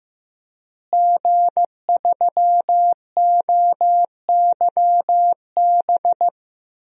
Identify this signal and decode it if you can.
Morse code, heard as a keyed tone, about 15 wpm, 700 Hz. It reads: G3OYB
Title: DXCC-100 Call Signs - 5 Characters - Repeated 2x - 15wpm